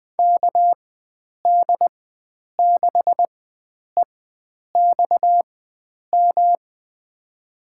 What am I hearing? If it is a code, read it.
Morse code, heard as a keyed tone, about 20 wpm, spaced as for 9 wpm, 700 Hz. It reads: KD6EXM
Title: USA Call Signs - 6 Characters - 4x Character Spacing - 20wpm